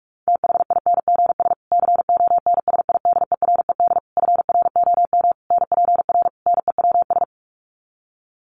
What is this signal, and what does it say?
Morse code, heard as a keyed tone, about 44 wpm, 700 Hz. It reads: THINGS CONSIDERED FROM NPR NEWS
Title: Sets of 5 Words Encore 45wpm